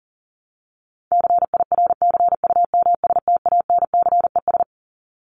Morse code, heard as a keyed tone, about 40 wpm, 700 Hz. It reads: CIRCUMSTANCES
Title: ES:S Suffix - Top 100 Words with Base Words 40wpm